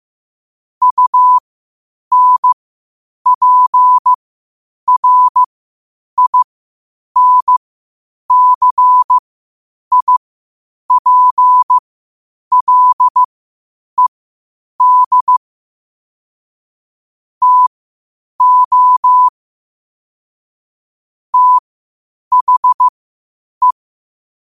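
Morse code, heard as a keyed tone, about 15 wpm, spaced as for 8 wpm, 1000 Hz. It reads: UNPRINCIPLED TO THE
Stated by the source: Morse code quotes at 8 WPM.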